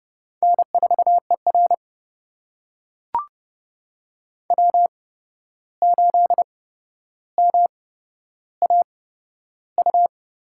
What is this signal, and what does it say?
Morse code, heard as a keyed tone, about 30 wpm, spaced as for 12 wpm, 700 Hz. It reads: N4ER W8MAU
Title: USA Call Signs - 8x Character Spacing - 30wpm